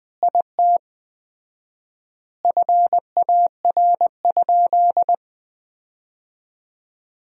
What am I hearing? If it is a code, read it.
Morse code, heard as a keyed tone, about 21 wpm, 700 Hz. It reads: IT FAR?
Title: 4x Word-Spacing - Sentences from Top 300 Words - 20wpm